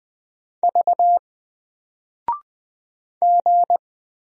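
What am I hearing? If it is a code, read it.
Morse code, heard as a keyed tone, about 20 wpm, 700 Hz. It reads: V G
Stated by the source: Single letter and number Morse drill at 20 wpm, from N4LCD.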